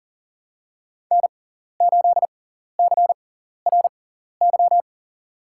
Morse code, heard as a keyed tone, about 40 wpm, spaced as for 14 wpm, 700 Hz. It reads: N8CRY
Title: USA Call Signs - 6x Character Spacing - 40wpm